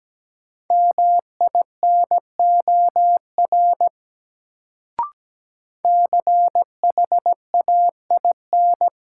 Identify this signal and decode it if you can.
Morse code, heard as a keyed tone, about 17 wpm, 700 Hz. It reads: MINOR CHAIN
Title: Sentences from Top 1500 Words - New Word Drill - 17wpm